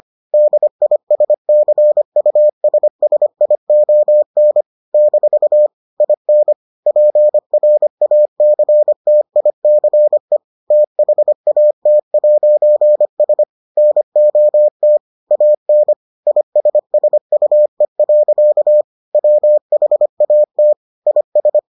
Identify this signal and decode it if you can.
Morse code, heard as a keyed tone, about 25 wpm, 600 Hz. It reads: DISCUSSION - IN PRACTICE THAT'S NOT AN ISSUE. WHAT IS